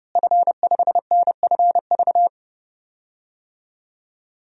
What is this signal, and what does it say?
Morse code, heard as a keyed tone, about 30 wpm, 700 Hz. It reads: F5NFV
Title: DXCC-100 Call Signs - 5 Characters - Repeated 3x - 30wpm